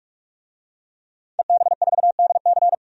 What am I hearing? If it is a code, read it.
Morse code, heard as a keyed tone, about 46 wpm, 700 Hz. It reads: EB4DC
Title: DXCC-100 Call Signs - 5 Characters - 45wpm